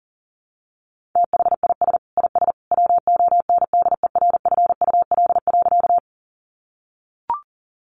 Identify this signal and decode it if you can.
Morse code, heard as a keyed tone, about 40 wpm, 700 Hz. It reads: THIS IS WONDERFUL.
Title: Sentences from Top 1100 Words 40wpm